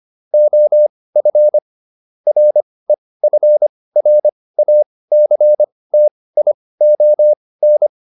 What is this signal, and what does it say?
Morse code, heard as a keyed tone, about 25 wpm, spaced as for 18 wpm, 600 Hz. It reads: OF REFRACTION